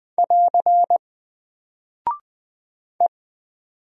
Morse code, heard as a keyed tone, about 20 wpm, 700 Hz. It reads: <AR> E
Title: Single Character (Letter:Number:Punctuation:Prosign) 20wpm